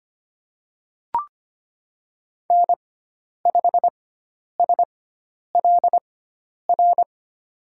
Morse code, heard as a keyed tone, about 25 wpm, spaced as for 10 wpm, 700 Hz. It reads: N5SLR
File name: Lesson 16g - Letter c - Call Signs 25wpm